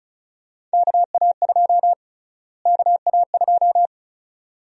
Morse code, heard as a keyed tone, about 36 wpm, 700 Hz. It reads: KA2 KA2
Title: USA Call Sign Prefixes - Repeated 2x - 35wpm